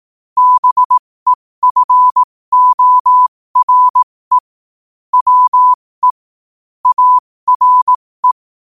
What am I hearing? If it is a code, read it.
Morse code, heard as a keyed tone, about 18 wpm, spaced as for 15 wpm, 1000 Hz. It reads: BEFORE WE ARE
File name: qotdfiles / 15wpm